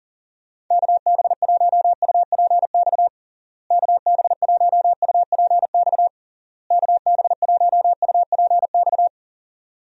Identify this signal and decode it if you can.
Morse code, heard as a keyed tone, about 40 wpm, 700 Hz. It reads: KB1UPX KB1UPX KB1UPX
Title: USA Call Signs - 6 Characters - Repeated 3x - 40wpm